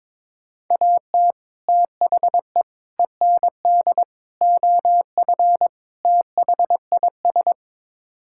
Morse code, heard as a keyed tone, about 22 wpm, 700 Hz. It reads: AT THE END OF THIS
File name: practice / 22wpm / Build A Phrase - 5-Word Phrases - 22wpm